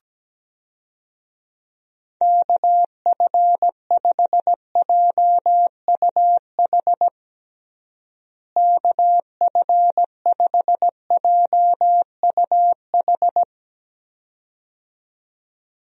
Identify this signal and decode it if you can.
Morse code, heard as a keyed tone, about 17 wpm, 700 Hz. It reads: KF5JUH KF5JUH
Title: USA Call Signs - 6 Characters - Repeated 2x - 17wpm